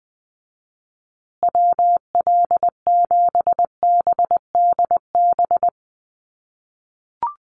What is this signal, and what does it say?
Morse code, heard as a keyed tone, about 19 wpm, 700 Hz. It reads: WL7BDB
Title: USA Call Signs Repeated 2x 20wpm